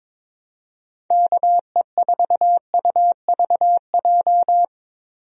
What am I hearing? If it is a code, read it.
Morse code, heard as a keyed tone, about 22 wpm, 700 Hz. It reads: KE4UVJ
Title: USA Call Signs - 6 Characters - 22wpm